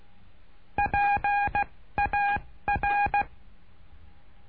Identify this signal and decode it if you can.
Morse code, heard as a keyed tone, about 16 wpm, 820 Hz. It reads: P A R
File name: cursos / CURSO DE TELEGRAFIA / Telegrafia - Trilha 09